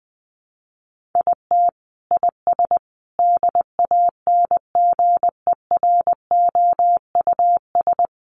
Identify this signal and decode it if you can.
Morse code, heard as a keyed tone, about 20 wpm, 700 Hz. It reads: IT IS DANGEROUS
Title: Sentences from Top 1200 Words 20wpm